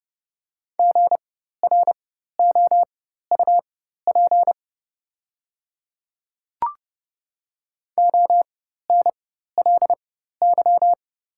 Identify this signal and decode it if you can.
Morse code, heard as a keyed tone, about 31 wpm, spaced as for 14 wpm, 700 Hz. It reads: GROUP ONLY
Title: 4x Character Spacing - Top 500 Words - 30wpm